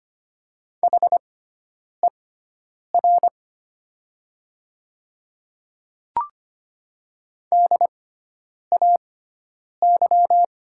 Morse code, heard as a keyed tone, about 25 wpm, spaced as for 9 wpm, 690 Hz. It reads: HER DAY
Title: 6x Character Spacing - Top 200 Words - 25wpm